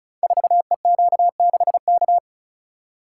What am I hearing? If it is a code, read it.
Morse code, heard as a keyed tone, about 36 wpm, 700 Hz. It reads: 4EQ6K
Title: Code Groups - 5 Letter:Numbers - 35wpm